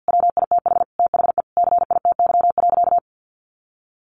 Morse code, heard as a keyed tone, about 50 wpm, 700 Hz. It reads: WITH THE CITY.
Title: Sentences from Top 1500 Words 50wpm